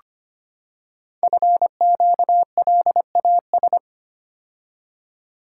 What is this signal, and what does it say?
Morse code, heard as a keyed tone, about 25 wpm, 700 Hz. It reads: FQLAS
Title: Code Groups - 5 Letters - 25wpm